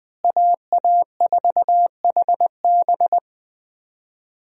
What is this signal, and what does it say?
Morse code, heard as a keyed tone, about 21 wpm, 700 Hz. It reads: AA4HB
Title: USA Call Signs - 5 Characters - Repeated 2x - 20wpm